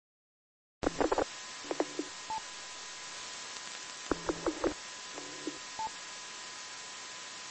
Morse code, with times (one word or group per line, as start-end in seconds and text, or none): none